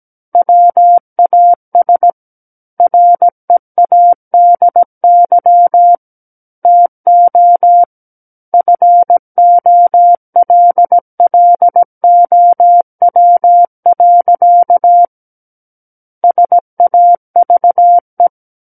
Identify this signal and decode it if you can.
Morse code, heard as a keyed tone, about 17 wpm, 700 Hz. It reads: WAS READY TO FOLLOW. SAVE
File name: Book One / 08-Chapter 8 - 17 WPM